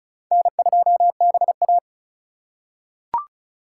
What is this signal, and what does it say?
Morse code, heard as a keyed tone, about 36 wpm, 700 Hz. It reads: N2BA
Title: USA Call Signs - 4 Characters - 3x Character Spacing - 35wpm